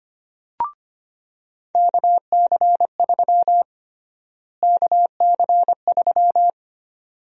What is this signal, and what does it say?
Morse code, heard as a keyed tone, about 25 wpm, 700 Hz. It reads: KC3 KC3
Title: USA Call Sign Prefixes - Repeated 2x - 25wpm